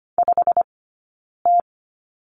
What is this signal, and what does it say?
Morse code, heard as a keyed tone, about 25 wpm, 700 Hz. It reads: 5 T
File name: Lesson 17d - Letter u - Two Characters 25wpm